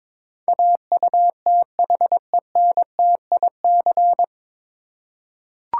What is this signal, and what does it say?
Morse code, heard as a keyed tone, about 22 wpm, 700 Hz. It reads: AUTHENTIC
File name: 9 Letter Words - First 3 Letters Given - 22wpm